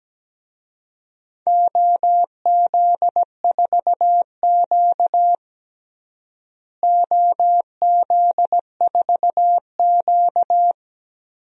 Morse code, heard as a keyed tone, about 17 wpm, 700 Hz. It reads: OZ4Q OZ4Q
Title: DXCC-100 Call Signs - 4 Characters - Repeated 3x - 17wpm